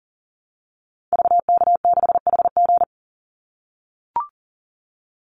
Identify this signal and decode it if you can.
Morse code, heard as a keyed tone, about 40 wpm, 700 Hz. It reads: VK6HG
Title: Call Signs - 40wpm